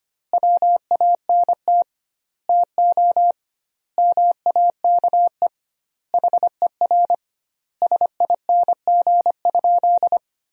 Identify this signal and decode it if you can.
Morse code, heard as a keyed tone, about 25 wpm, 700 Hz. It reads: WANT TO MAKE HER SING?
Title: 2x Word-Spacing - Sentences from Top 400 Words - 25wpm